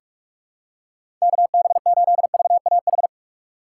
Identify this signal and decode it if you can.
Morse code, heard as a keyed tone, about 47 wpm, 700 Hz. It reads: KB8VAH
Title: USA Call Signs - 3x Character Spacing - 45wpm